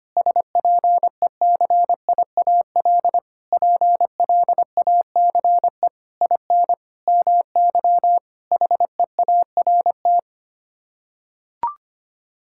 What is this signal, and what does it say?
Morse code, heard as a keyed tone, about 25 wpm, 700 Hz. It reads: SPECIAL PLACE IN MY HEART
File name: Sets of 9 Words - 25wpm